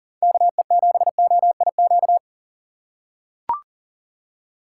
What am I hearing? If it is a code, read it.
Morse code, heard as a keyed tone, about 37 wpm, 700 Hz. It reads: KE7OIQ
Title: USA Call Signs - 6 Characters - 6x Character Spacing - 40wpm